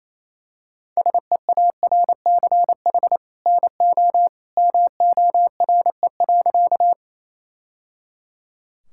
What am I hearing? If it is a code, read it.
Morse code, heard as a keyed tone, about 28 wpm, 700 Hz. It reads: SEARCH NO MORE.
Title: Sentences from Top 1000 Words - Review Part 5 - 28wpm